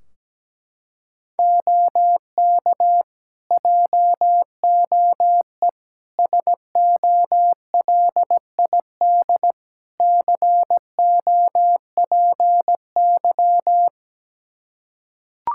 Morse code, heard as a keyed tone, about 17 wpm, 700 Hz. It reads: OK JOE SOLID COPY
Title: Practice QSO - Exchange 3 - More Info - 17wpm